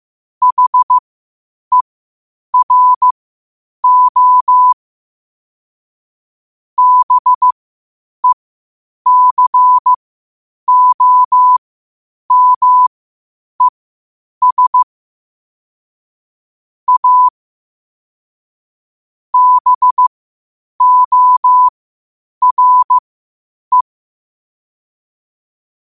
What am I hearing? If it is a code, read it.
Morse code, heard as a keyed tone, about 15 wpm, spaced as for 8 wpm, 1000 Hz. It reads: HERO BECOMES A BORE
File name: qotdfiles / 8wpm